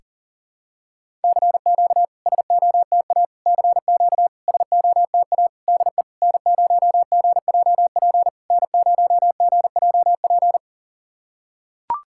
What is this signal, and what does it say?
Morse code, heard as a keyed tone, about 42 wpm, 700 Hz. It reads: CQ SOTA CQ SOTA DE N0GJP N0GJP
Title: SOTA QSO Practice 40wpm